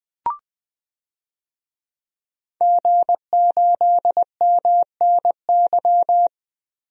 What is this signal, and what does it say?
Morse code, heard as a keyed tone, about 20 wpm, 700 Hz. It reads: G8MNY
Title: DXCC-100 Call Signs - 5 Characters - 20wpm